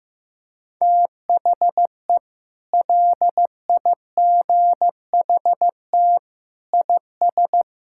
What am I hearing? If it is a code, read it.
Morse code, heard as a keyed tone, about 15 wpm, 700 Hz. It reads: THE LIGHT IS
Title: Sentences from Top 1000 Words - Review Part 6 - 15wpm